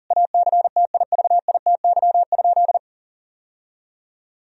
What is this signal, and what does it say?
Morse code, heard as a keyed tone, about 40 wpm, 700 Hz. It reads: ACTIVITY?
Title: Sentences from Top 1000 Words - Review Part 4 - 40wpm